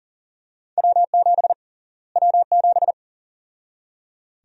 Morse code, heard as a keyed tone, about 38 wpm, 700 Hz. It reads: W7 W7
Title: USA Call Sign Prefixes - Repeated 2x - 40wpm